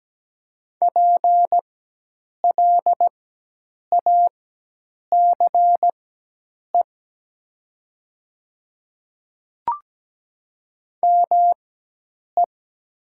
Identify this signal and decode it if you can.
Morse code, heard as a keyed tone, about 17 wpm, spaced as for 8 wpm, 700 Hz. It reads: PLACE ME
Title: 4x Character Spacing - Top 200 Words - 17wpm